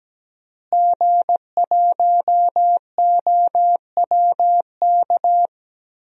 Morse code, heard as a keyed tone, about 17 wpm, 700 Hz. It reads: G1OWK
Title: DXCC-100 Call Signs - 5 Characters - Repeated 3x - 17wpm